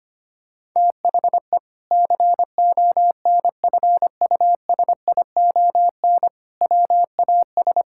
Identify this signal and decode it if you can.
Morse code, heard as a keyed tone, about 25 wpm, 700 Hz. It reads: THE CONFUSION WAS
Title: Sentences from Top 1600 Words 25wpm